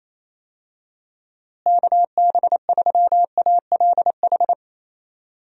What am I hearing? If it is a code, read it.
Morse code, heard as a keyed tone, about 28 wpm, 700 Hz. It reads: KB3ALH
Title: USA Call Signs - 6 Characters - Repeated 3x - 28wpm